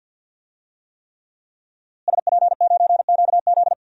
Morse code, heard as a keyed tone, about 52 wpm, 700 Hz. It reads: SP9QZ
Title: DXCC-100 Call Signs - 5 Characters - Repeated 3x - 50wpm